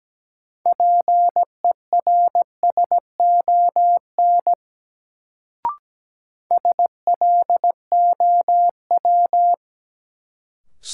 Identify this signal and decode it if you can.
Morse code, heard as a keyed tone, about 17 wpm, 700 Hz. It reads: PERSON SLOW
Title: Sentences from Top 400 Words - New Word Drill - 17wpm